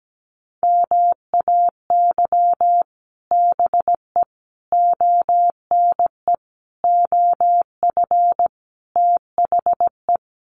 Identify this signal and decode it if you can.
Morse code, heard as a keyed tone, about 17 wpm, 700 Hz. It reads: MAY BE ONE OF THE
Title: Sets of 5 Words Encore2 17wpm